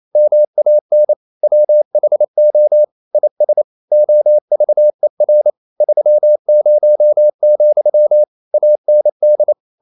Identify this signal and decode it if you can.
Morse code, heard as a keyed tone, about 28 wpm, 600 Hz. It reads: MAN WHO IS OVER 30, AND
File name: CW / Practice / EnglishQuotations0228